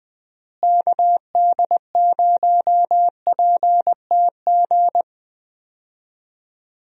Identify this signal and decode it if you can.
Morse code, heard as a keyed tone, about 20 wpm, 700 Hz. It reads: KD0PTG
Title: USA Call Signs - 6 Characters - Repeated 3x - 20wpm